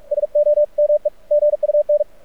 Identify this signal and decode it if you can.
Morse code, heard as a keyed tone, about 44 wpm, 590 Hz. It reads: SOME GUN
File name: archived / CRSnet